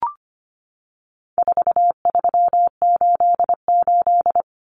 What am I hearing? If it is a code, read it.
Morse code, heard as a keyed tone, about 25 wpm, 700 Hz. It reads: 4388
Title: Common Cut Numbers - 4-Digit 25wpm